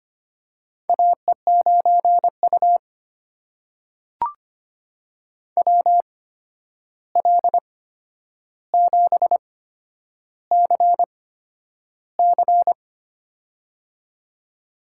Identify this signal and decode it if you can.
Morse code, heard as a keyed tone, about 25 wpm, spaced as for 10 wpm, 700 Hz. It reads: AE9U WL7CC
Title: USA Call Signs - 8x Character Spacing - 25wpm